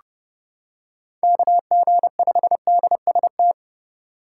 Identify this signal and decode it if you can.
Morse code, heard as a keyed tone, about 28 wpm, 700 Hz. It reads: KG5DST